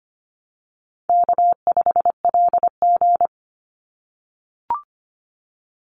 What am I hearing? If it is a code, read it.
Morse code, heard as a keyed tone, about 25 wpm, 700 Hz. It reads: K5LG
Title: Call Signs - 25wpm